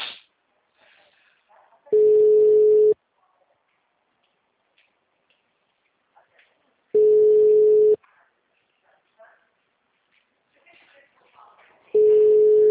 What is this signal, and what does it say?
Ringback tone (Europe, 425 Hz), with speech.
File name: out- -2043- - - .30